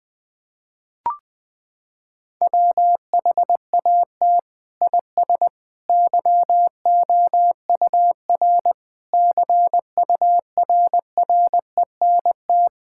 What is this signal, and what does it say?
Morse code, heard as a keyed tone, about 21 wpm, 700 Hz. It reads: WHAT IS YOUR CURRENT
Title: Sentences from Top 1000 Words - Review Part 6 - 20wpm